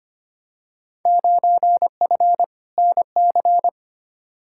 Morse code, heard as a keyed tone, about 25 wpm, 700 Hz. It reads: 9F NC
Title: Field Day Exchanges Encore - 25wpm